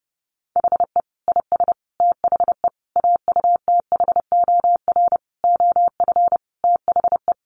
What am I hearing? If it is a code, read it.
Morse code, heard as a keyed tone, about 30 wpm, 700 Hz. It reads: HE IS THE AUTHOR OF THE
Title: Sentences from Top 1200 Words 30wpm